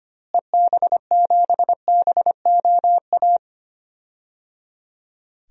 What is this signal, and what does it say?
Morse code, heard as a keyed tone, about 25 wpm, 700 Hz. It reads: EB7BOA
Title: DXCC-100 Call Signs - 6 Characters - Repeated 3x - 25wpm